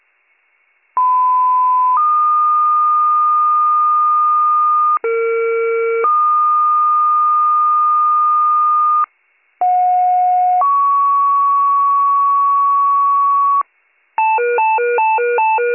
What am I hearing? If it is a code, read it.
Two-tone page, heard as a one-second tone, then a three-second tone, about 1010 then 1220 Hz, about 460 then 1220 Hz, about 730 then 1120 Hz. It is heard from 1.0 to 13.6 s.